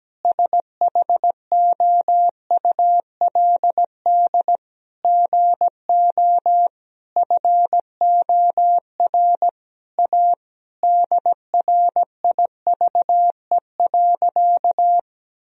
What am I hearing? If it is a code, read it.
Morse code, heard as a keyed tone, about 17 wpm, 700 Hz. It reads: SHOULD GO FOR A DRIVE.
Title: Sentences from Top 1000 Words - Review Part 2 - 17wpm